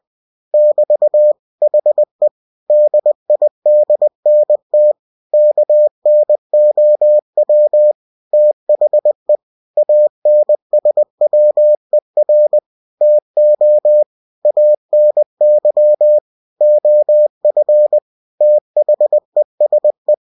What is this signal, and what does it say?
Morse code, heard as a keyed tone, about 20 wpm, 600 Hz. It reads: <BT> HE DIDNT KNOW THE ANSWER TO ANY OF THESE
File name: cw-books / winnie-the-pooh / winnie-20wpm-page-0041